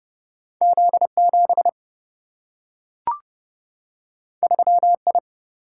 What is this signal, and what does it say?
Morse code, heard as a keyed tone, about 30 wpm, 700 Hz. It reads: Z7 3I
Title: Single-Letter Single-Number 30wpm